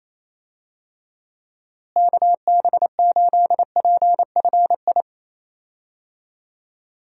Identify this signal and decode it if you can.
Morse code, heard as a keyed tone, about 27 wpm, 700 Hz. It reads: KB8PFI
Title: USA Call Signs - 6 Characters - Repeated 2x - 28wpm